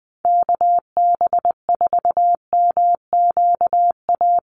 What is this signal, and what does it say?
Morse code, heard as a keyed tone, about 19 wpm, 700 Hz. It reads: KB4MQA
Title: USA Call Signs Repeated 3x 20wpm